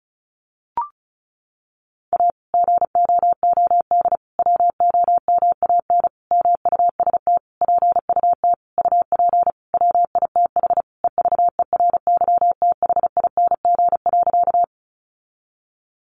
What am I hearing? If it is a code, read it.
Morse code, heard as a keyed tone, about 34 wpm, 700 Hz. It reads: A GOOD WOMAN MUST PUT UP WITH EVERYTHING.
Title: Sentences from Top 1100 Words 35wpm